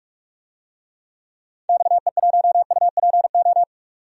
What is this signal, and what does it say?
Morse code, heard as a keyed tone, about 46 wpm, 700 Hz. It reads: XE1UPO
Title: DXCC-100 Call Signs - 6 Characters - Repeated 3x - 45wpm